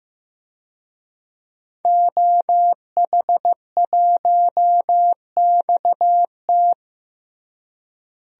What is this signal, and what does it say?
Morse code, heard as a keyed tone, about 15 wpm, 700 Hz. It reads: OH1XT
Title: DXCC-100 Call Signs - 5 Characters - Repeated 3x - 15wpm